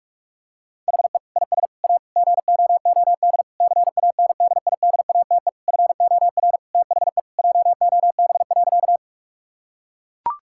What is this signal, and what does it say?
Morse code, heard as a keyed tone, about 41 wpm, 700 Hz. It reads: HE IS A GOOD CANDIDATE FOR THE JOB.
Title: Sentences from Top 1600 Words 45wpm